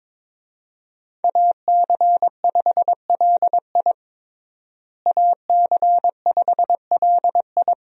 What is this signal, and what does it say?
Morse code, heard as a keyed tone, about 22 wpm, 700 Hz. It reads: AC5LI AC5LI
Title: USA Call Signs - 5 Characters - Repeated 2x - 22wpm